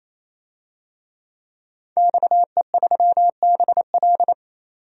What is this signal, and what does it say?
Morse code, heard as a keyed tone, about 28 wpm, 700 Hz. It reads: XE3BL
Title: DXCC-100 Call Signs - 5 Characters - 28wpm